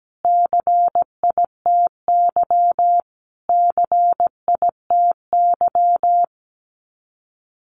Morse code, heard as a keyed tone, about 17 wpm, 700 Hz. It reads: CITY CITY
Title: Top 500 Words Repeated 17wpm